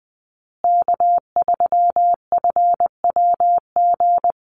Morse code, heard as a keyed tone, about 20 wpm, 700 Hz. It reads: K3FWG
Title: USA Call Signs Repeated 3x 20wpm